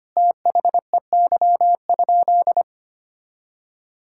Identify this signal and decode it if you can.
Morse code, heard as a keyed tone, about 25 wpm, 700 Hz. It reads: THEY?
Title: Sentences from Top 1000 Words - Review Part 5 - 25wpm